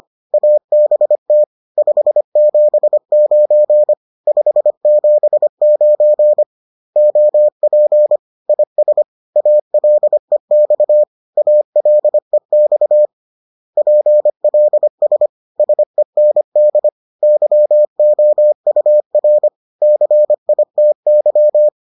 Morse code, heard as a keyed tone, about 25 wpm, 600 Hz. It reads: ABT 579 579 OP IS ALEX ALEX PLS SEND YOUR CITY